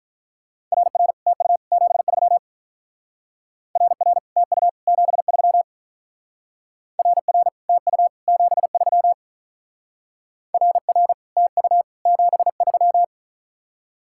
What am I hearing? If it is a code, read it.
Morse code, heard as a keyed tone, about 44 wpm, 700 Hz. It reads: RR TU 73 RR TU 73 RR TU 73 RR TU 73
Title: Speed-Racing Common QSO Phrases - 35wpm